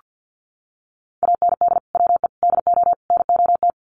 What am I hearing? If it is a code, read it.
Morse code, heard as a keyed tone, about 50 wpm, 700 Hz. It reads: AND WE DO NOT
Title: Sets of 4 Words 50wpm